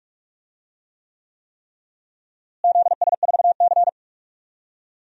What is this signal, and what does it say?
Morse code, heard as a keyed tone, about 46 wpm, 700 Hz. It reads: ZS4C
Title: DXCC-100 Call Signs - 4 Characters - Repeated 3x - 45wpm